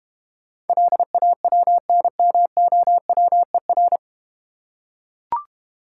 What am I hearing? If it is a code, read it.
Morse code, heard as a keyed tone, about 31 wpm, 700 Hz. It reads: LAWNMOWER
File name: Easy Words - Speed-Racing with Hints - 25wpm